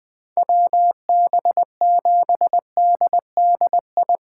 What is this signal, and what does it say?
Morse code, heard as a keyed tone, about 20 wpm, 700 Hz. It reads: WB7DDI
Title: USA Call Signs - 6 Characters - 4x Character Spacing - 20wpm